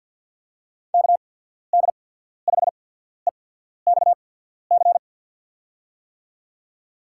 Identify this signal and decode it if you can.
Morse code, heard as a keyed tone, about 50 wpm, spaced as for 14 wpm, 700 Hz. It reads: KD5EXC
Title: USA Call Signs - 6 Characters - 8x Character Spacing - 50wpm